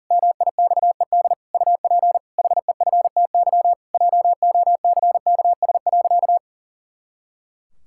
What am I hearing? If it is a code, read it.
Morse code, heard as a keyed tone, about 40 wpm, 700 Hz. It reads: MIXED UP HEFTY JOCKS.
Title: Pangram Challenge v2 - 40wpm